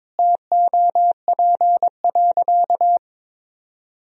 Morse code, heard as a keyed tone, about 21 wpm, 700 Hz. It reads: TOP.
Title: Sentences from Top 1000 Words - Review Part 2 - 22wpm